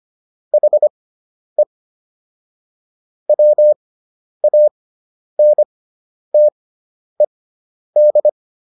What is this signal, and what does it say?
Morse code, heard as a keyed tone, about 25 wpm, spaced as for 10 wpm, 600 Hz. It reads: HE WANTED